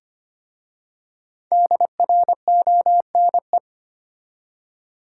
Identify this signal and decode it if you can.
Morse code, heard as a keyed tone, about 25 wpm, 700 Hz. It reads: DRONE
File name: practice / 25wpm / Difficult Words with Hints - 25wpm